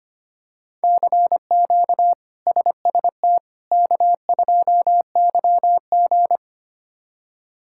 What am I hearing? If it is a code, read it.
Morse code, heard as a keyed tone, about 25 wpm, 700 Hz. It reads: CQ SST K2YG
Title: K1USN SST QSO Practice - 25wpm